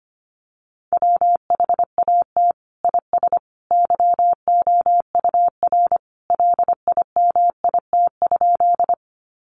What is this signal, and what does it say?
Morse code, heard as a keyed tone, about 25 wpm, 700 Hz. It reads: WHAT IS YOUR LIMIT?
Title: Sentences from Top 1500 Words 25wpm